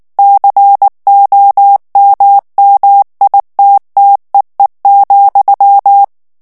Morse code, heard as a keyed tone, about 19 wpm, 800 Hz. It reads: COMMITTEE,
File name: morse mp3 / ANGLAIS texte clair 3 vitesse 18 mots minute